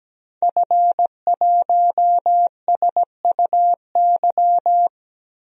Morse code, heard as a keyed tone, about 17 wpm, 700 Hz. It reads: F1SUY
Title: DXCC-100 Call Signs - 5 Characters - Repeated 2x - 17wpm